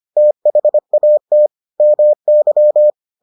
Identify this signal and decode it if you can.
Morse code, heard as a keyed tone, about 25 wpm, 600 Hz. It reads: THAT MY